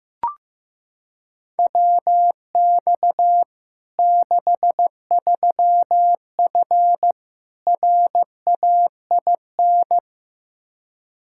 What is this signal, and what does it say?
Morse code, heard as a keyed tone, about 15 wpm, 700 Hz. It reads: WX 63F RAIN
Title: Practice QSO - Exchange 2 - Info - 15wpm